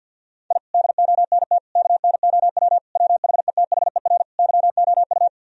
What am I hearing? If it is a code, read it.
Morse code, heard as a keyed tone, about 46 wpm, 700 Hz. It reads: I DONT KNOW WHETHER YOU
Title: Sentences from Top 1000 Words - Review Part 3 - 50wpm